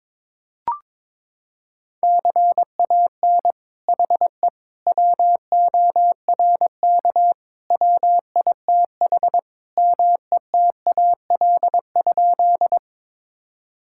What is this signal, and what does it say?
Morse code, heard as a keyed tone, about 22 wpm, 700 Hz. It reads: CAN HE WORK WITH METAL?
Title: Sentences from Top 1000 Words - Review Part 6 - 22wpm